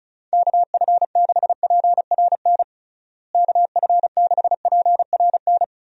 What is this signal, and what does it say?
Morse code, heard as a keyed tone, about 34 wpm, 700 Hz. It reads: KF6PRN KF6PRN
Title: USA Call Signs - 6 Characters - Repeated 2x - 35wpm